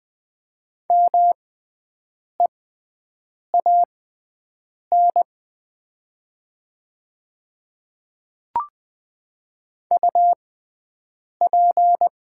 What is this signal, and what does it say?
Morse code, heard as a keyed tone, about 20 wpm, spaced as for 7 wpm, 700 Hz. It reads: MEAN UP
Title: 6x Character Spacing - Top 200 Words - 20wpm